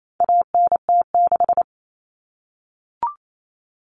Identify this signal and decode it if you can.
Morse code, heard as a keyed tone, about 28 wpm, 700 Hz. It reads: ANT6
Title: Common Cut Numbers - 4-Digit 28wpm